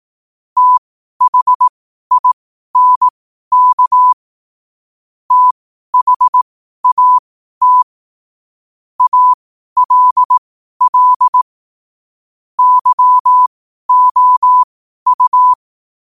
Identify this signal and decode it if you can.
Morse code, heard as a keyed tone, about 18 wpm, spaced as for 12 wpm, 1000 Hz. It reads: THINK THAT ALL YOU
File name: qotdfiles / 12wpm